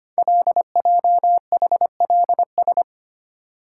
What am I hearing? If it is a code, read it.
Morse code, heard as a keyed tone, about 25 wpm, 700 Hz. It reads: LJHLS
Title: Code Groups - 5 Letters - 25wpm